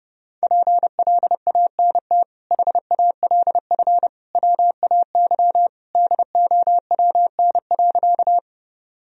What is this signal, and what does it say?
Morse code, heard as a keyed tone, about 32 wpm, 700 Hz. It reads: PLANT HALF WAY DOWN.
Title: Sentences from Top 1000 Words - Review Part 1 - 30wpm